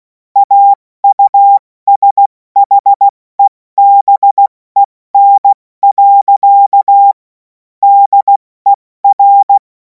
Morse code, heard as a keyed tone, about 16 wpm, 800 Hz. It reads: AUSHEBEN. DER
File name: Morsekurs / Witze / Witz 17. Februar 2026 70 F80